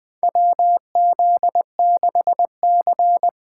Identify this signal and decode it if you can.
Morse code, heard as a keyed tone, about 20 wpm, 700 Hz. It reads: WZ6C
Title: USA Call Signs - 4 Characters - 8x Character Spacing - 20wpm